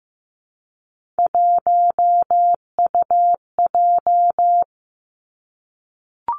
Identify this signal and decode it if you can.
Morse code, heard as a keyed tone, about 15 wpm, 700 Hz. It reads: 1UJ
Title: Three Challenging Letter-Number Combinations 15wpm